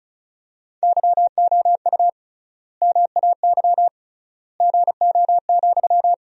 Morse code, heard as a keyed tone, about 34 wpm, 700 Hz. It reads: YOU MAY GO,
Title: 3x Word-Spacing - Sentences from Top 400 Words - 35wpm